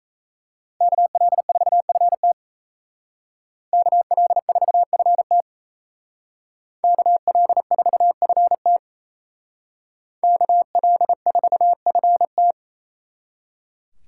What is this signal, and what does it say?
Morse code, heard as a keyed tone, about 36 wpm, 700 Hz. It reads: KL4FT KL4FT KL4FT KL4FT